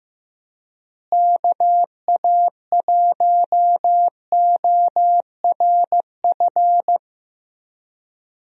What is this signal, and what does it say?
Morse code, heard as a keyed tone, about 15 wpm, 700 Hz. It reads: KA1ORF
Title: USA Call Signs - 6 Characters - Repeated 3x - 15wpm